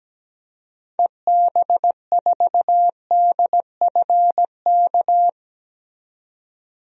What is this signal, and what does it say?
Morse code, heard as a keyed tone, about 17 wpm, 700 Hz. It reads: EB4DFK
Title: DXCC-100 Call Signs - 6 Characters - 17wpm